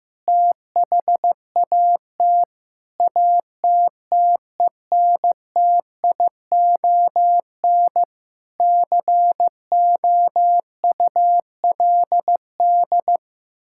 Morse code, heard as a keyed tone, about 15 wpm, 700 Hz. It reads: THAT ATTENTION COULD
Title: Sentences from Top 1000 Words - Review Part 6 - 15wpm